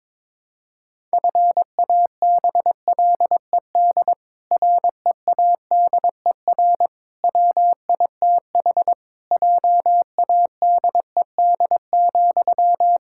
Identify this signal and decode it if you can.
Morse code, heard as a keyed tone, about 22 wpm, 700 Hz. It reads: FABLED READER WITH JADED,
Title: Pangram Challenge v2 - 22wpm